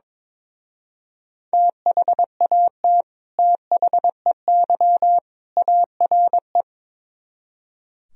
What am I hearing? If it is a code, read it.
Morse code, heard as a keyed tone, about 22 wpm, 700 Hz. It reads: THAT THEY ARE
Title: Speed-Racing - Sets of 3 Words 22wpm